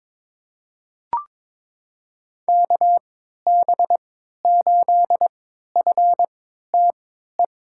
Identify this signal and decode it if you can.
Morse code, heard as a keyed tone, about 22 wpm, spaced as for 12 wpm, 700 Hz. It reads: KB8FTE
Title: USA Call Signs - 3x Character Spacing - 22wpm